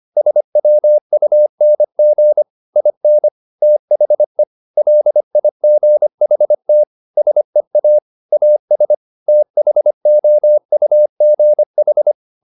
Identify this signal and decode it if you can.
Morse code, heard as a keyed tone, about 25 wpm, 600 Hz. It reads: SWUNG IN THE LIGHT SEA AS THOUGH